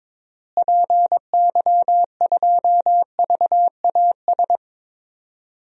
Morse code, heard as a keyed tone, about 22 wpm, 700 Hz. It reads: PY2VAS
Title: DXCC-100 Call Signs - 6 Characters - Repeated 2x - 22wpm